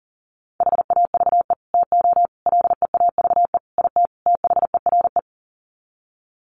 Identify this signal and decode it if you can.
Morse code, heard as a keyed tone, about 40 wpm, 700 Hz. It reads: HAVE TO LEAVE IT THERE
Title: Sets of 5 Words Encore2 40wpm